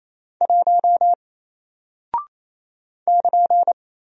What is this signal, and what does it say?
Morse code, heard as a keyed tone, about 27 wpm, 700 Hz. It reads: 1 <KN>
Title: Single Character (Letter:Number:Punctuation:Prosign) 28wpm